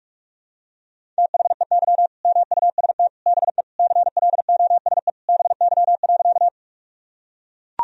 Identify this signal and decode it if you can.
Morse code, heard as a keyed tone, about 41 wpm, 700 Hz. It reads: THEY MUST BE CLOSE BY.
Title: Sentences from Top 1000 Words - Review Part 1 - 45wpm